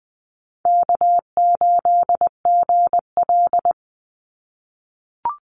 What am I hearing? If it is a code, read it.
Morse code, heard as a keyed tone, about 20 wpm, 700 Hz. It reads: K8GL
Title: Call Signs - 20wpm